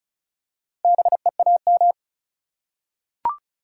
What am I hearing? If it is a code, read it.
Morse code, heard as a keyed tone, about 36 wpm, 700 Hz. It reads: BEAM
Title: Top 180 QSO Elements 35wpm